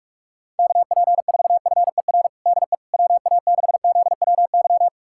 Morse code, heard as a keyed tone, about 46 wpm, 690 Hz. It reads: KP4FER DE WA6ZWY
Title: USA Callsign DE Callsign - 6 Characters - 45wpm